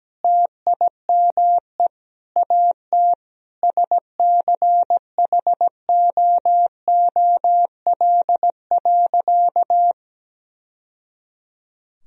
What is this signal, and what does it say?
Morse code, heard as a keyed tone, about 17 wpm, 700 Hz. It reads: TIME AT SCHOOL.
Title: Sentences from Top 1000 Words - Review Part 1 - 17wpm